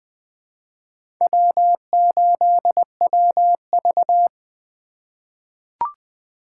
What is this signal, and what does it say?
Morse code, heard as a keyed tone, about 20 wpm, 700 Hz. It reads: W8WV
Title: USA Call Signs - 8x Character Spacing - 20wpm